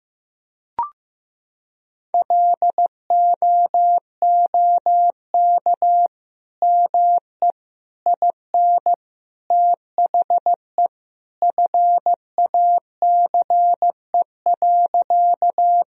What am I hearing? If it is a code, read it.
Morse code, heard as a keyed tone, about 15 wpm, 700 Hz. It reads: LOOK ME IN THE FACE.
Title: Sentences from Top 1000 Words - Review Part 2 - 15wpm